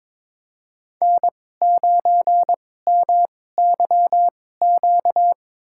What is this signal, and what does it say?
Morse code, heard as a keyed tone, about 23 wpm, spaced as for 16 wpm, 700 Hz. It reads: N9MYQ
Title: USA Call Signs - 5 Characters - 2x Character Spacing - 22wpm